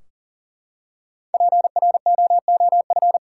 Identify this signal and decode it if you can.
Morse code, heard as a keyed tone, about 40 wpm, 700 Hz. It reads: PROOF
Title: Five Letter Words - 40wpm